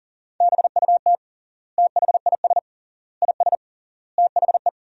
Morse code, heard as a keyed tone, about 40 wpm, 700 Hz. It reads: BUT THIS IS THE
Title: 3x Word-Spacing - Sentences from Top 200 Words - 40wpm